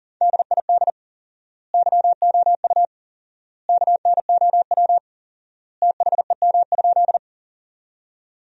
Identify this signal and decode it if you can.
Morse code, heard as a keyed tone, about 40 wpm, 700 Hz. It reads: DID YOU KNOW THEM?
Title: 4x Word-Spacing - Sentences from Top 100 Words - 40wpm